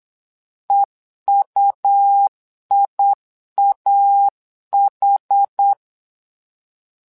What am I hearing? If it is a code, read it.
Morse code, heard as a keyed tone, about 8 wpm, 800 Hz. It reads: EUIAH
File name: LEZIONI / 2 UHDN / 2 UH 8wpm 2129